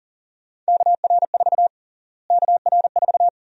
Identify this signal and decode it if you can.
Morse code, heard as a keyed tone, about 40 wpm, 700 Hz. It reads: KR4 KR4
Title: USA Call Sign Prefixes - Repeated 2x - 40wpm